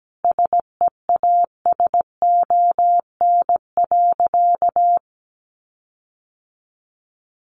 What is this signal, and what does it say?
Morse code, heard as a keyed tone, about 17 wpm, 700 Hz. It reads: SEASON.
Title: Sentences from Top 1400 Words 17wpm